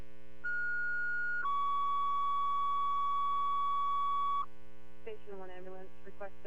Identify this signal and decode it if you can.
Two-tone page, heard as a one-second tone, then a three-second tone, about 1440 then 1130 Hz, with speech. It is heard from 0.4 to 4.4 s.